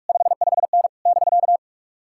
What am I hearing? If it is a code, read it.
Morse code, heard as a keyed tone, about 59 wpm, 700 Hz. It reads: 55N <BK>